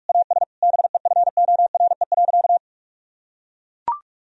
Morse code, heard as a keyed tone, about 42 wpm, 700 Hz. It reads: AS BEFORE.
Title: Sentences from Top 1000 Words - Review Part 2 - 45wpm